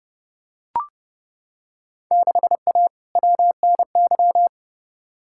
Morse code, heard as a keyed tone, about 30 wpm, 700 Hz. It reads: 6A WNY
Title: Field Day Exchanges Encore - 30wpm